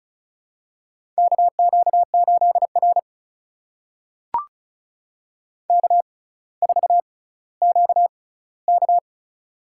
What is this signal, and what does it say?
Morse code, heard as a keyed tone, about 36 wpm, spaced as for 18 wpm, 700 Hz. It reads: KQ8R K4QK
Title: USA Call Signs - 6x Character Spacing - 35wpm